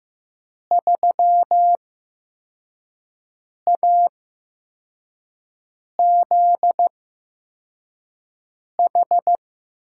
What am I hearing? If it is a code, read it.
Morse code, heard as a keyed tone, about 15 wpm, spaced as for 4 wpm, 700 Hz. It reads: 3AZH
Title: USA Call Signs - 5 Characters - 8x Character Spacing - 15wpm